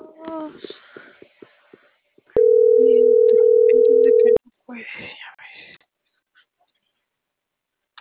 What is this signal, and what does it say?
Ringback tone (North America), 440 + 480 Hz.
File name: CENTRAL NMP / out marisol.monzalvonmp